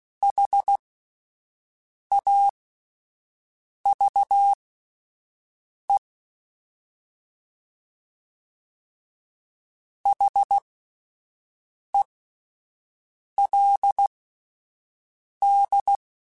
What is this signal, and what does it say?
Morse code, heard as a keyed tone, about 16 wpm, spaced as for 5 wpm, 800 Hz. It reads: HAVE HELD
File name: QSOs Spaced